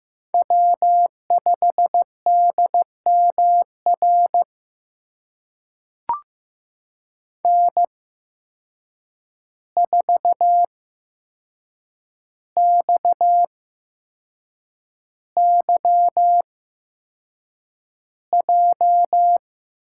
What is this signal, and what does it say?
Morse code, heard as a keyed tone, about 15 wpm, spaced as for 6 wpm, 700 Hz. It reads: W5DMR N4XYJ
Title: USA Call Signs - 5 Characters - 8x Character Spacing - 15wpm